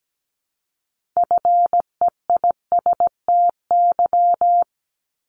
Morse code, heard as a keyed tone, about 17 wpm, 700 Hz. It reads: FEISTY
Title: Multi-letter Phonemes Repeated 3x 17wpm